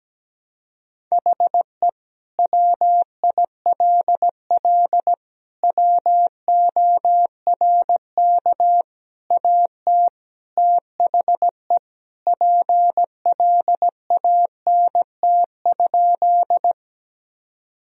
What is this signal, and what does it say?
Morse code, heard as a keyed tone, about 17 wpm, 700 Hz. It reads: HE WILL WORK AT THE PLANT?
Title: Sentences from Top 1000 Words - Review Part 1 - 17wpm